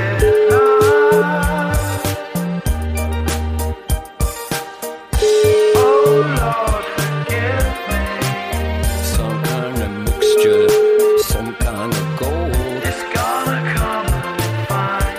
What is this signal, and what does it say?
Ringback tone (Europe, 425 Hz), with noise.